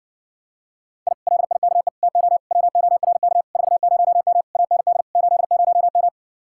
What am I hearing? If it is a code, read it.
Morse code, heard as a keyed tone, about 50 wpm, 700 Hz. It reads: I LIKE TO WORK 40M AND 80M
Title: Speed-Racing Common QSO Phrases - 40wpm